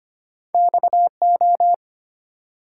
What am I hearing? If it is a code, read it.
Morse code, heard as a keyed tone, about 25 wpm, 700 Hz. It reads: XO
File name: Two Letter-Number Combinations - 25wpm